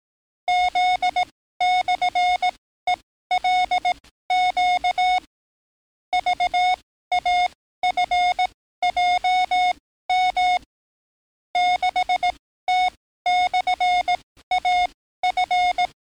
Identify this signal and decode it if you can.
Morse code, heard as a keyed tone, about 18 wpm, spaced as for 13 wpm, 730 Hz. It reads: Z/ELQ VAFJM 6T/AF
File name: training / NAVPERS 11049to11060 / NAVPERS 11056 RA RB